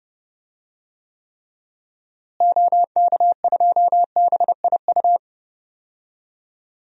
Morse code, heard as a keyed tone, about 32 wpm, 700 Hz. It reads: OK2BIU
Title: DXCC-100 Call Signs - 6 Characters - Repeated 2x - 30wpm